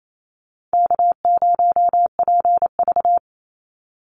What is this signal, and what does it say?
Morse code, heard as a keyed tone, about 28 wpm, 700 Hz. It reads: K0PV
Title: Call Signs Repeated 2x - 28wpm